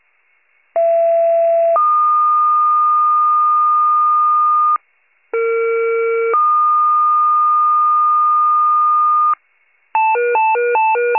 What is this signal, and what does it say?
Two-tone page, heard as a one-second tone, then a three-second tone, about 670 then 1220 Hz, about 460 then 1220 Hz. It is heard from 0.8 to 9.3 s.